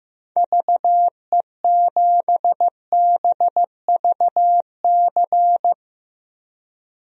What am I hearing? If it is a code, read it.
Morse code, heard as a keyed tone, about 15 wpm, 700 Hz. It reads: VE7BVC
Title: DXCC-100 Call Signs - 6 Characters - Repeated 2x - 15wpm